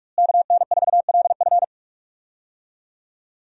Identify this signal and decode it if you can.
Morse code, heard as a keyed tone, about 41 wpm, 700 Hz. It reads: KN4LF
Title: USA Call Signs - 5 Characters - Repeated 2x - 45wpm